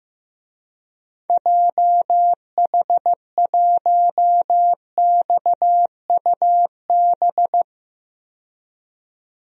Morse code, heard as a keyed tone, about 15 wpm, 700 Hz. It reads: JH1XUB
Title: DXCC-100 Call Signs - 6 Characters - 15wpm